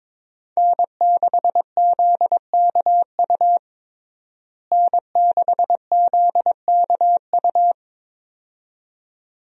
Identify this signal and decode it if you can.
Morse code, heard as a keyed tone, about 22 wpm, 700 Hz. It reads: N6ZKU N6ZKU
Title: USA Call Signs - 5 Characters - Repeated 2x - 22wpm